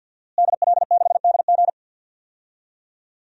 Morse code, heard as a keyed tone, about 47 wpm, 700 Hz. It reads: DL6DG
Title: DXCC-100 Call Signs - 5 Characters - Repeated 3x - 50wpm